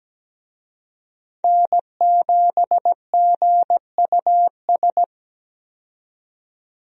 Morse code, heard as a keyed tone, about 17 wpm, 700 Hz. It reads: N7GUS
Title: USA Call Signs - 5 Characters - 17wpm